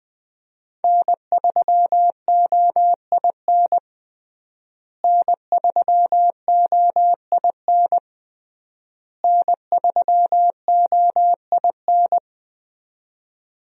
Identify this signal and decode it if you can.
Morse code, heard as a keyed tone, about 20 wpm, 700 Hz. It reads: N3OIN N3OIN N3OIN
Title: USA Call Signs - 5 Characters - Repeated 3x - 20wpm